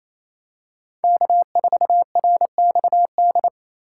Morse code, heard as a keyed tone, about 27 wpm, 700 Hz. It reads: K4RXD
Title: USA Call Signs - 5 Characters - 4x Character Spacing - 28wpm